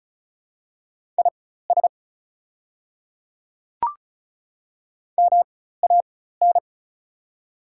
Morse code, heard as a keyed tone, about 36 wpm, spaced as for 16 wpm, 700 Hz. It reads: IS MAN
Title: 4x Character Spacing - Top 300 Words - 35wpm